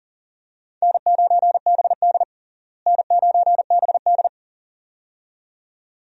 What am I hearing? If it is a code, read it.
Morse code, heard as a keyed tone, about 40 wpm, 700 Hz. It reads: N9BD N9BD
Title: USA Call Signs - 4 Characters - Repeated 2x - 40wpm